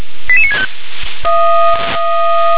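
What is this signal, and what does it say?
Five-tone selective call (ZVEI), heard as quick runs of five tones, tones of about 70 ms. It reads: 88965
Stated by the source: Five-tone ZVEI selective call received by SDR.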